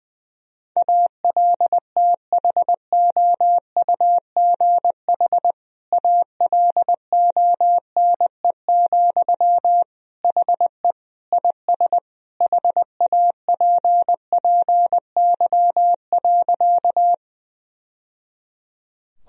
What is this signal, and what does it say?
Morse code, heard as a keyed tone, about 20 wpm, 700 Hz. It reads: ALTHOUGH ALONE, HE IS HAPPY.
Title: Sentences from Top 1000 Words - Review Part 3 - 20wpm